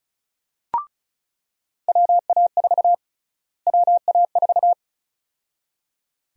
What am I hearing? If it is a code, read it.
Morse code, heard as a keyed tone, about 36 wpm, 700 Hz. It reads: WA4 WA4
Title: USA Call Sign Prefixes - Repeated 2x - 35wpm